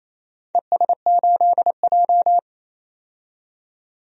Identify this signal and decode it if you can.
Morse code, heard as a keyed tone, about 28 wpm, 700 Hz. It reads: ES8J
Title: DXCC-100 Call Signs - 4 Characters - 28wpm